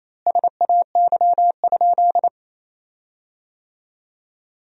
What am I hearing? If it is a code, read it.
Morse code, heard as a keyed tone, about 28 wpm, 700 Hz. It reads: SAY?
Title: 4x Word-Spacing - Sentences from Top 200 Words - 28wpm